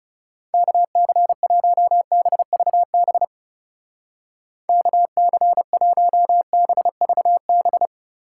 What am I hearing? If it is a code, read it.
Morse code, heard as a keyed tone, about 35 wpm, 700 Hz. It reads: KC1BVB KC1BVB
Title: Speed Racing - USA Call Signs 30wpm